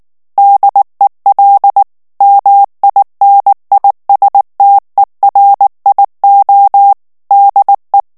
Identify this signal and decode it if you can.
Morse code, heard as a keyed tone, about 19 wpm, 800 Hz. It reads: DEL MINISTERIO DE
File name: morse mp3 / ESPAGNOL TEXTE EN CLAIR 2 VITESSE 18 MOTS MINUTES